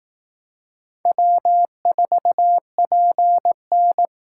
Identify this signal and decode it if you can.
Morse code, heard as a keyed tone, about 18 wpm, 700 Hz. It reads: W4PN
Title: Speed Racing - USA 1x2 Call Signs 15wpm